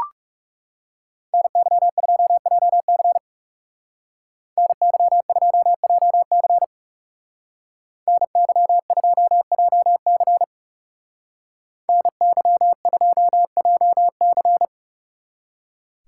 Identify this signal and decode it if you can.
Morse code, heard as a keyed tone, about 35 wpm, 700 Hz. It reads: NY2JC NY2JC NY2JC NY2JC
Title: Speed Racing - USA Call Signs 30wpm